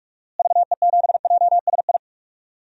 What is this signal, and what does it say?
Morse code, heard as a keyed tone, about 46 wpm, 700 Hz. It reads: VE7JSI
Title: DXCC-100 Call Signs - 6 Characters - Repeated 3x - 45wpm